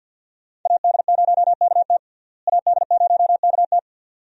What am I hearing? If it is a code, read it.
Morse code, heard as a keyed tone, about 53 wpm, 700 Hz. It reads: AD0KT AD0KT
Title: USA Call Signs - 5 Characters - Repeated 2x - 50wpm